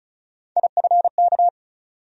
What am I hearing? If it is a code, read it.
Morse code, heard as a keyed tone, about 36 wpm, 700 Hz. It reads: IFK
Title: USA Call Sign Postfixes - 3 Letters 35wpm